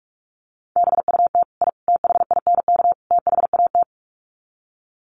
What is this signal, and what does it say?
Morse code, heard as a keyed tone, about 44 wpm, 700 Hz. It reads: BUT I THINK THAT
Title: Sets of 4 Words 45wpm